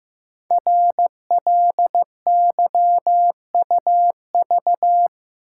Morse code, heard as a keyed tone, about 15 wpm, 700 Hz. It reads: RLYUV
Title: Code Groups - 5 Letters - 15wpm